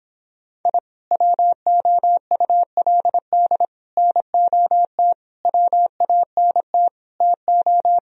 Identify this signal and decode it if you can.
Morse code, heard as a keyed tone, about 26 wpm, 700 Hz. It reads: I WOULD NOT WANT TO